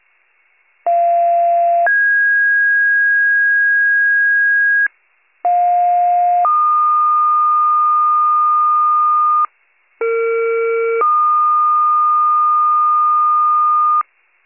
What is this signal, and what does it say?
Two-tone page, heard as a one-second tone, then a three-second tone, about 690 then 1650 Hz, about 710 then 1220 Hz, about 460 then 1220 Hz. It is heard from 0.9 to 14.0 s.